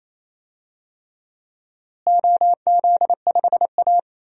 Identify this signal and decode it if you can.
Morse code, heard as a keyed tone, about 28 wpm, 700 Hz. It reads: OZ5A
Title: DXCC-100 Call Signs - 4 Characters - Repeated 3x - 28wpm